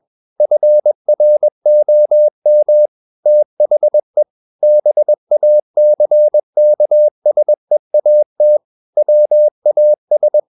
Morse code, heard as a keyed tone, about 21 wpm, 600 Hz. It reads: FROM THE BACKSEAT WAS
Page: CW med Gnister